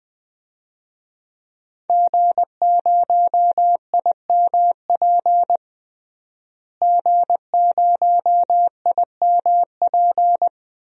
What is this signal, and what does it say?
Morse code, heard as a keyed tone, about 20 wpm, 700 Hz. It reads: G0IMP G0IMP
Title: DXCC-100 Call Signs - 5 Characters - Repeated 2x - 20wpm